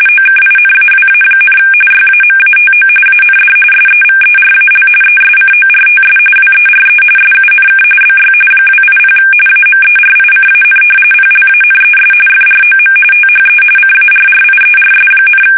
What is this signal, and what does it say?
RTTY, about 45.45 baud. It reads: OKINAWA, YAMATO AND SEVERAL ESCORTS WERE SUNK BY 280 AMERICAN PLANES FROM THE CARRIERS,
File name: fsk850 / battleshipnj